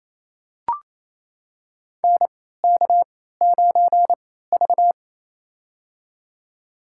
Morse code, heard as a keyed tone, about 28 wpm, spaced as for 16 wpm, 700 Hz. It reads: NK9V
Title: USA Call Signs - 4 Characters - 3x Character Spacing - 28wpm